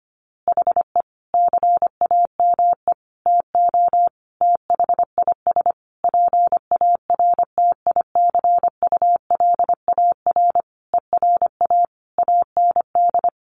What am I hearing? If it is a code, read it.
Morse code, heard as a keyed tone, about 25 wpm, 700 Hz. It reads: HE CAME TO THIS PARTICULAR ERA AND
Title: A Sound Of Thunder 25wpm